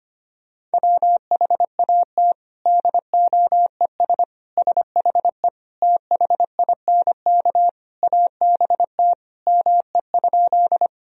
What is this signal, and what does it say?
Morse code, heard as a keyed tone, about 25 wpm, 700 Hz. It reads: WHAT DOES SHE THINK ABT ME?
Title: Sentences from Top 200 Words and Abbreviations 25wpm